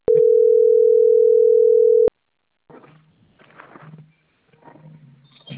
Ringback tone (North America), 440 + 480 Hz.